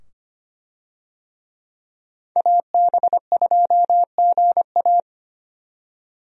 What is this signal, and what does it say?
Morse code, heard as a keyed tone, about 25 wpm, 700 Hz. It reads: AB2GA
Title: USA Call Signs - 5 Characters - 25wpm